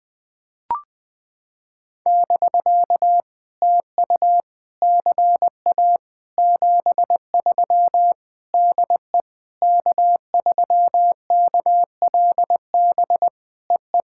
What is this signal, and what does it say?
Morse code, heard as a keyed tone, about 18 wpm, 700 Hz. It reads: <BK> TU CA 73 DE K3KLB EE
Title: POTA QSO Practice - 20wpm